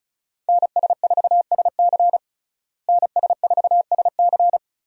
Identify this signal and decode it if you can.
Morse code, heard as a keyed tone, about 35 wpm, 700 Hz. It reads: NS4SC NS4SC
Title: USA Call Signs - 5 Characters - Repeated 2x - 35wpm